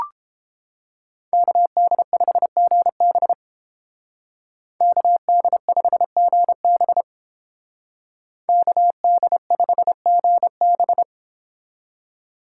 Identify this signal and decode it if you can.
Morse code, heard as a keyed tone, about 30 wpm, 700 Hz. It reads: KD5GB KD5GB KD5GB